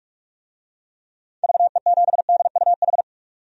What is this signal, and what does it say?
Morse code, heard as a keyed tone, about 46 wpm, 700 Hz. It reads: VE7DUH
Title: DXCC-100 Call Signs - 6 Characters - 45wpm